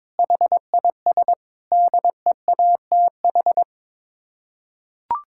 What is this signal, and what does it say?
Morse code, heard as a keyed tone, about 22 wpm, 700 Hz. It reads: HIS DEATH
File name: practice / 22wpm / Sets of 6 Words - 22wpm